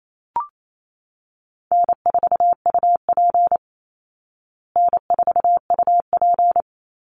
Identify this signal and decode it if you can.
Morse code, heard as a keyed tone, about 28 wpm, 700 Hz. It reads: N4UP N4UP
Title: Call Signs Repeated 2x - 28wpm